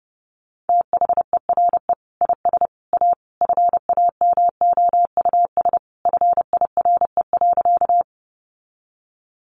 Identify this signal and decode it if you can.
Morse code, heard as a keyed tone, about 30 wpm, 700 Hz. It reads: THERE IS A FAMOUS FIRE.
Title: Sentences from Top 1500 Words 30wpm